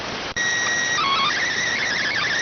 SSTV (slow-scan television), PD 290.